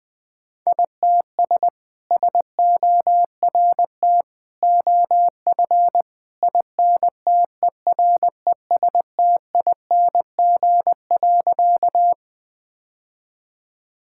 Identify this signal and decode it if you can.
Morse code, heard as a keyed tone, about 20 wpm, 700 Hz. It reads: ITS SORT OF INTERESTING.
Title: Sentences with 2-Word Phrases - 20wpm